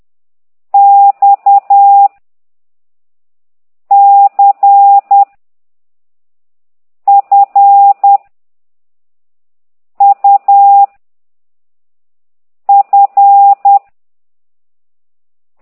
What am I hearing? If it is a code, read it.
Morse code, heard as a keyed tone, about 10 wpm, 800 Hz. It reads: X C F U F